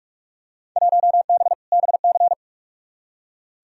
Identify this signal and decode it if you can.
Morse code, heard as a keyed tone, about 43 wpm, 700 Hz. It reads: 1B BC
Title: Field Day Exchanges - 45wpm